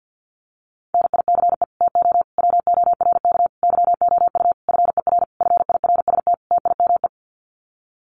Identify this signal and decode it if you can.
Morse code, heard as a keyed tone, about 50 wpm, 700 Hz. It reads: NICE TO WORK YOU FER FIRST TIME
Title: Common QSO Phrases 50wpm